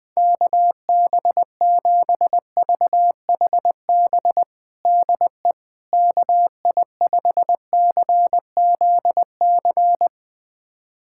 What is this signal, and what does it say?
Morse code, heard as a keyed tone, about 20 wpm, 700 Hz. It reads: KB7VHB DE KI5CZC
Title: USA Callsign DE Callsign - 6 Characters - 20wpm